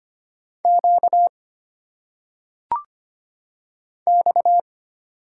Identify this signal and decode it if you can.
Morse code, heard as a keyed tone, about 25 wpm, 700 Hz. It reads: Q X
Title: Single Letters - Phonetic Pronunciation 25wpm